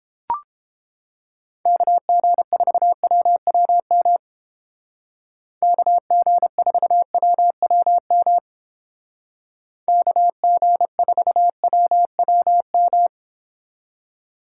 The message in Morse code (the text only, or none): KG4WWM KG4WWM KG4WWM